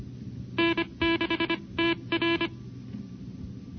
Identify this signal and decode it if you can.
Morse code, heard as a keyed tone, about 25 wpm, spaced as for 22 wpm, 330 Hz. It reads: N6TR
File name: NCCC sweepstakes